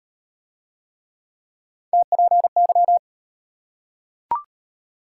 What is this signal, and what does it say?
Morse code, heard as a keyed tone, about 38 wpm, 700 Hz. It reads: TPY